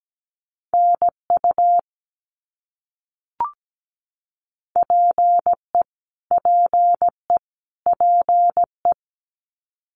Morse code, heard as a keyed tone, about 17 wpm, 700 Hz. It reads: NU PE PE PE
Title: Canadian Provinces and Territories - Repeated 3x - 17wpm